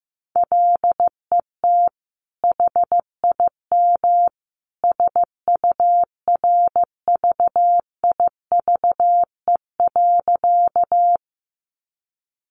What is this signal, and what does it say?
Morse code, heard as a keyed tone, about 15 wpm, 700 Hz. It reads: LET HIM SURVIVE.
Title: Sentences from Top 1300 Words 15wpm